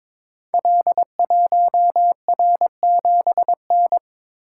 Morse code, heard as a keyed tone, about 22 wpm, 700 Hz. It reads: L1R7N
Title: Code Groups - 5 Letter:Numbers - 22wpm